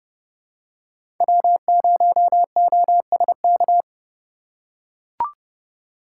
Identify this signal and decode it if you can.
Morse code, heard as a keyed tone, about 29 wpm, 700 Hz. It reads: W0OSK
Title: USA Call Signs - 3x Character Spacing - 30wpm